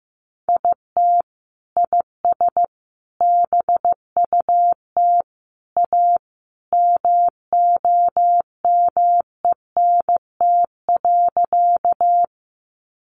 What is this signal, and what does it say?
Morse code, heard as a keyed tone, about 15 wpm, 700 Hz. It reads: IT IS BUT A MOMENT.
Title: Sentences from Top 1000 Words 15wpm